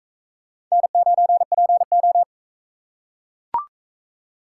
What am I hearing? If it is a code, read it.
Morse code, heard as a keyed tone, about 39 wpm, 700 Hz. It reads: N9PO